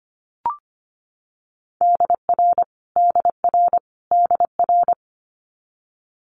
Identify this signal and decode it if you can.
Morse code, heard as a keyed tone, about 25 wpm, 700 Hz. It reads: DR DR DR
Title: Multi-letter Phonemes Repeated 3x 25wpm